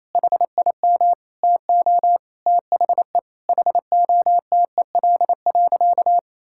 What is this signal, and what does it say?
Morse code, heard as a keyed tone, about 28 wpm, 700 Hz. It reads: HIM TO THE HOTEL.
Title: Sentences from Top 1600 Words 28wpm